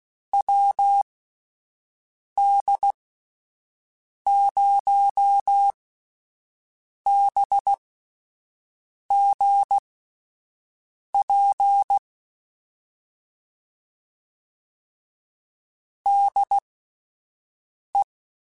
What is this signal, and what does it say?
Morse code, heard as a keyed tone, about 16 wpm, spaced as for 5 wpm, 800 Hz. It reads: WD0BGP DE
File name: (spaced) / QSOs Spaced-mp3s / WD0BGP 1 @10 sp